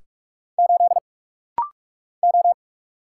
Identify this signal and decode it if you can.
Morse code, heard as a keyed tone, about 46 wpm, 700 Hz. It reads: 8 O
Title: Single Letter-Number - Rapid-Fire 45wpm